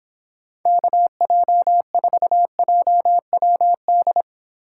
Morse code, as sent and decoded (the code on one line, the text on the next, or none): -.- .--- ....- .--- .-- -..
KJ4JWD